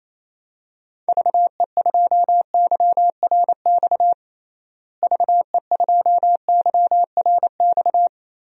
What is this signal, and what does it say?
Morse code, heard as a keyed tone, about 28 wpm, 700 Hz. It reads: VE2YRX VE2YRX
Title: DXCC-100 Call Signs - 6 Characters - Repeated 3x - 28wpm